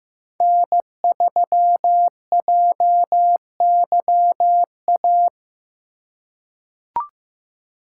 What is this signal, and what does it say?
Morse code, heard as a keyed tone, about 15 wpm, 700 Hz. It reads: N3JYA
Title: USA Call Signs - 5 Characters - 2x Character Spacing - 15wpm